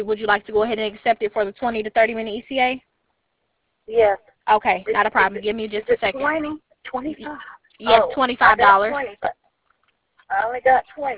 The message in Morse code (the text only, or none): none